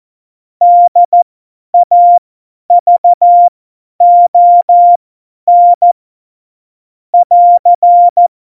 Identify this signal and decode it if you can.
Morse code, heard as a keyed tone, about 14 wpm, spaced as for 10 wpm, 700 Hz. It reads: DAVON <AR>
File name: Morsekurs / Witze / Witz 08. Dezember 2020, 50 F70